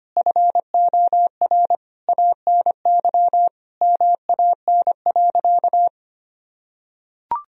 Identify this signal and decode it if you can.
Morse code, heard as a keyed tone, about 25 wpm, 700 Hz. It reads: FOR ANY MAN.
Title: Sentences from Top 1000 Words - Review Part 3 - 25wpm